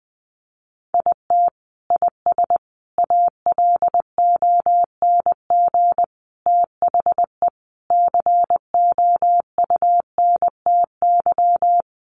Morse code, heard as a keyed tone, about 20 wpm, 700 Hz. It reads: IT IS ALONG THE COUNTY
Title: Sentences from Top 1200 Words 20wpm